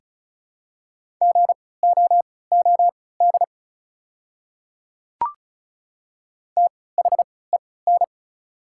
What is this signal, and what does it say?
Morse code, heard as a keyed tone, about 36 wpm, spaced as for 20 wpm, 700 Hz. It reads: GOOD THEN
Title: 3x Character Spacing - Top 200 Words - 35wpm